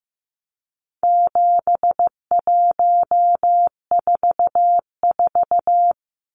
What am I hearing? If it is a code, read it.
Morse code, heard as a keyed tone, about 15 wpm, 700 Hz. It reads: 7144
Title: Numbers - 4-Digit 15wpm